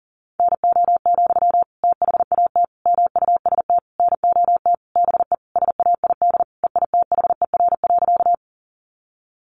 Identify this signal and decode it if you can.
Morse code, heard as a keyed tone, about 40 wpm, 700 Hz. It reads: NO, THAT MUST NOT BE SAID EITHER.
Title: Sentences from Top 600 Words 40wpm